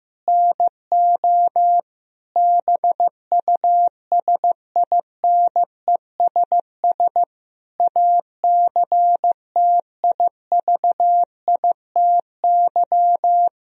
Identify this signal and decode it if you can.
Morse code, heard as a keyed tone, about 15 wpm, 700 Hz. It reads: NO BUSINESS ACTIVITY
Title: Sentences from Top 1000 Words - Review Part 4 - 15wpm